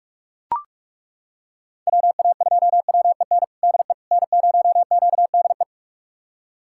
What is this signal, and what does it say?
Morse code, heard as a keyed tone, about 41 wpm, 700 Hz. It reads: WA2WEN DE N0QDE
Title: USA Callsign DE Callsign 45wpm